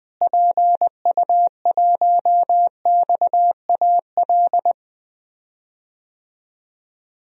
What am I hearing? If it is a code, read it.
Morse code, heard as a keyed tone, about 20 wpm, 700 Hz. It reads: PU1XAL
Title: DXCC-100 Call Signs - 6 Characters - Repeated 3x - 20wpm